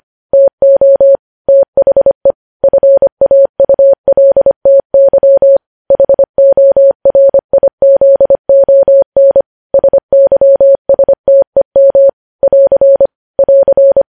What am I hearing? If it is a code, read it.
Morse code, heard as a keyed tone, about 25 wpm, 550 Hz. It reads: TO THE FAULTY HORIZON SYSTEM <AR> <AR>
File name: podcast / News headlines-1x-25x25